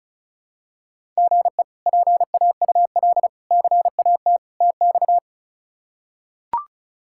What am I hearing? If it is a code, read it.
Morse code, heard as a keyed tone, about 36 wpm, 700 Hz. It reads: GE PAUL CAT TX
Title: K1USN SST QSO Practice - 35wpm